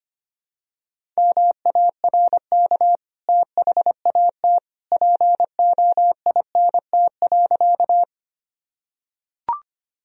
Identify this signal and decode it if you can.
Morse code, heard as a keyed tone, about 25 wpm, 700 Hz. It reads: MARK THAT POINT.
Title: Sentences from Top 1000 Words - Review Part 1 - 25wpm